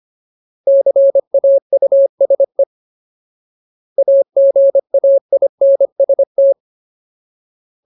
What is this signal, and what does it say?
Morse code, heard as a keyed tone, about 25 wpm, 550 Hz. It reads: CAUSE AGAINST
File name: Ultimate Word List / 25WPM Ultimate